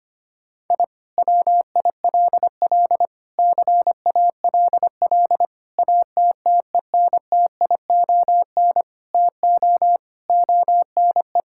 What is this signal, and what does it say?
Morse code, heard as a keyed tone, about 25 wpm, 700 Hz. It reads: I WILL CALL ATTENTION TO ONE
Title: Sentences from Top 1000 Words - Review Part 6 - 25wpm